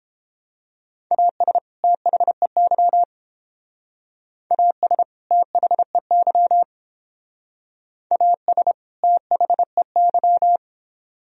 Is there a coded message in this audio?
Yes: Morse code from 1.1 to 10.6 s, heard as a keyed tone, about 31 wpm, 700 Hz. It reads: AS THEY AS THEY AS THEY